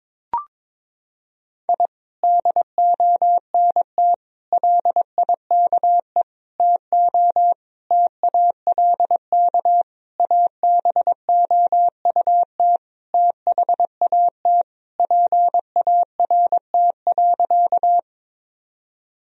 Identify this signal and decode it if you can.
Morse code, heard as a keyed tone, about 22 wpm, 700 Hz. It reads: I DONT LIKE TO TALK ABOUT THAT PART.
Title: Sentences from Top 1000 Words - Review Part 1 - 22wpm